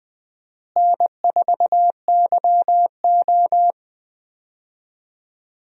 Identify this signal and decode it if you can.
Morse code, heard as a keyed tone, about 20 wpm, 700 Hz. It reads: N4YO
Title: USA Call Signs - 4 Characters - Repeated 3x - 20wpm